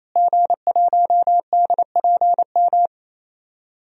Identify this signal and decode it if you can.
Morse code, heard as a keyed tone, about 28 wpm, 700 Hz. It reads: G1DPM
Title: DXCC-100 Call Signs - 5 Characters - Repeated 3x - 28wpm